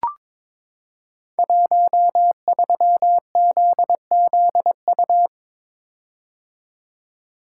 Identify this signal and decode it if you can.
Morse code, heard as a keyed tone, about 22 wpm, 700 Hz. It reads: 13ZZU
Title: Code Groups - 5 Letter:Numbers - 22wpm